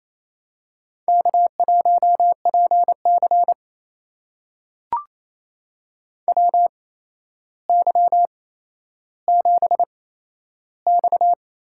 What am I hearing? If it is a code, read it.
Morse code, heard as a keyed tone, about 28 wpm, spaced as for 12 wpm, 700 Hz. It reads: K1PC WY7X
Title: USA Call Signs - 4 Characters - 8x Character Spacing - 28wpm